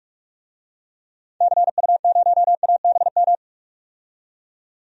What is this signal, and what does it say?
Morse code, heard as a keyed tone, about 44 wpm, 700 Hz. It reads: CU0ABM
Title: DXCC-100 Call Signs - 6 Characters - 45wpm